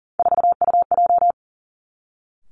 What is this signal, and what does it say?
Morse code, heard as a keyed tone, about 42 wpm, 700 Hz. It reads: 4UJ
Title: Three Challenging Letter-Number Combinations 40wpm